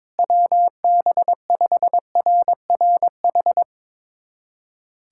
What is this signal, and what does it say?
Morse code, heard as a keyed tone, about 22 wpm, 700 Hz. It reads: WB5RRH
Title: USA Callsign DE Callsign - 6 Characters - 22wpm